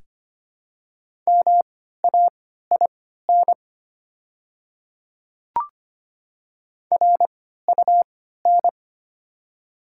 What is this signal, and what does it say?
Morse code, heard as a keyed tone, about 25 wpm, spaced as for 14 wpm, 700 Hz. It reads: MAIN RUN
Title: 3x Character Spacing - Top 300 Words - 25wpm